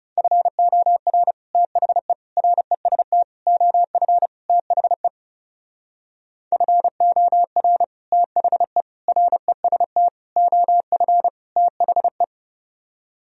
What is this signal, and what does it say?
Morse code, heard as a keyed tone, about 40 wpm, 700 Hz. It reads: FOR THE REST OF THE FOR THE REST OF THE